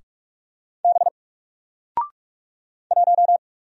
Morse code, heard as a keyed tone, about 45 wpm, 700 Hz. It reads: B 1
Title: Single Character (Letter:Number:Punctuation:Prosign) Alt 45wpm